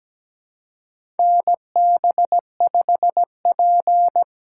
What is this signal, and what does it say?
Morse code, heard as a keyed tone, about 17 wpm, 700 Hz. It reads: NB5P
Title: USA Call Signs - 4 Characters - 4x Character Spacing - 17wpm